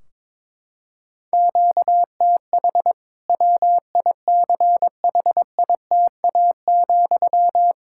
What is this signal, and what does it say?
Morse code, heard as a keyed tone, about 22 wpm, 700 Hz. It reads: QTH WICHITA,
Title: K1USN SST QSO Practice 22wpm